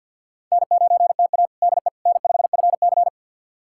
Morse code, heard as a keyed tone, about 53 wpm, 700 Hz. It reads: N9TA DE N5FC
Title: USA Callsign DE Callsign - 4 Characters - 50wpm